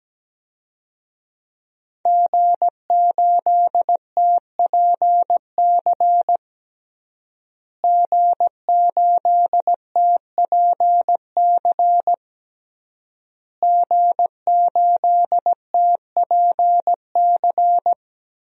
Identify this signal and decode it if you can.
Morse code, heard as a keyed tone, about 17 wpm, 700 Hz. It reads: G8TPC G8TPC G8TPC
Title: DXCC-100 Call Signs - 5 Characters - Repeated 3x - 17wpm